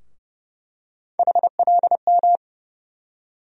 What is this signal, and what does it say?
Morse code, heard as a keyed tone, about 30 wpm, 700 Hz. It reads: HLM
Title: USA Call Sign Postfixes - 3 Letters 30wpm